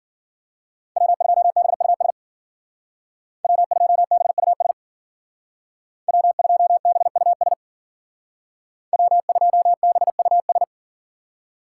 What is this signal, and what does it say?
Morse code, heard as a keyed tone, about 48 wpm, 700 Hz. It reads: W2BUS W2BUS W2BUS W2BUS
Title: Speed Racing - USA 1x3 Call Signs 40wpm